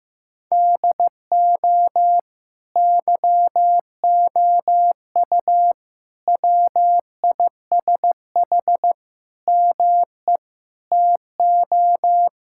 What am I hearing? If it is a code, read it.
Morse code, heard as a keyed tone, about 15 wpm, 700 Hz. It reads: DO YOU WISH ME TO
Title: Sentences from Top 1600 Words 15wpm